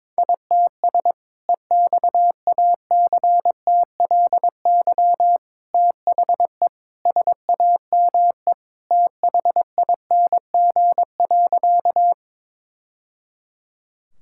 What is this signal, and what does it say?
Morse code, heard as a keyed tone, about 21 wpm, 700 Hz. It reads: ITS EXACTLY THE SAME THING.
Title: Sentences from Top 1000 Words - Review Part 6 - 22wpm